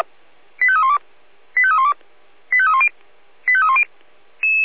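Five-tone selective call (ZVEI), heard as quick runs of five tones, tones of about 70 ms. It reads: 85312 85312 85310 85310
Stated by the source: ZVEI five-tone selective call received on an SDR.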